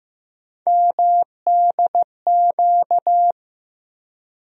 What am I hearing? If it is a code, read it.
Morse code, heard as a keyed tone, about 15 wpm, 700 Hz. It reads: MDQ
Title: USA Call Sign Postfixes - 3 Letters 15wpm